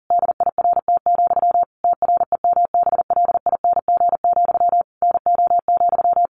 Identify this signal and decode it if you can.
Morse code, heard as a keyed tone, about 40 wpm, 700 Hz. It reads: DIRT, TREMBLING, NO,
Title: A Sound Of Thunder 40wpm